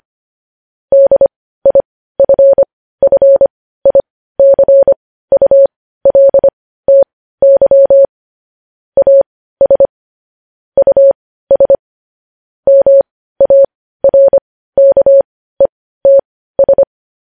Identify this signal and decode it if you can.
Morse code, heard as a keyed tone, about 25 wpm, spaced as for 15 wpm, 550 Hz. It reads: DIFFICULTY AS US MARKETS
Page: news-headlines-in-morse-code-at-15-wpm